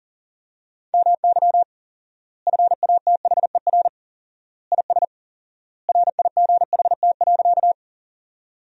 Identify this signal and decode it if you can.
Morse code, heard as a keyed tone, about 40 wpm, 700 Hz. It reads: MY FATHER IS RIGHT.
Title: 4x Word-Spacing - Sentences from Top 300 Words - 40wpm